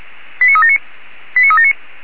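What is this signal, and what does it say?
Five-tone selective call (ZVEI), heard as quick runs of five tones, tones of about 70 ms. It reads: 89379 89379
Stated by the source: ZVEI five-tone selective call received by SDR.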